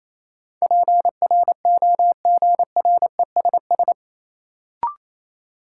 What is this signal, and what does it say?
Morse code, heard as a keyed tone, about 28 wpm, 700 Hz. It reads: PROGRESS
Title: Sentences from Top 1400 Words - New Word Drill Alt - 28wpm